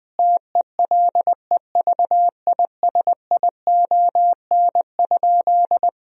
Morse code, heard as a keyed tone, about 20 wpm, 700 Hz. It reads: TELEVISION?
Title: Sentences from Top 1000 Words - Review Part 4 - 20wpm